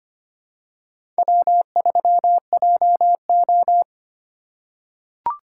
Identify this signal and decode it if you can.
Morse code, heard as a keyed tone, about 25 wpm, 700 Hz. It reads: W3JO
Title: USA Call Signs - 4 Characters - 2x Character Spacing - 25wpm